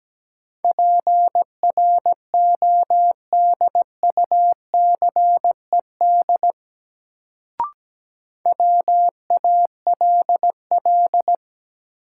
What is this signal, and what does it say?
Morse code, heard as a keyed tone, about 17 wpm, 700 Hz. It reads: PRODUCED WALL
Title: Sentences from Top 1000 Words - Review Part 6 - New Word Drill - 17wpm